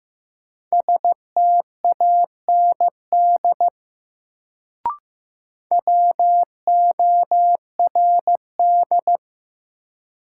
Morse code, heard as a keyed tone, about 15 wpm, 700 Hz. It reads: STAND WORD
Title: Sentences from Top 300 Words - New Word Drill - 15wpm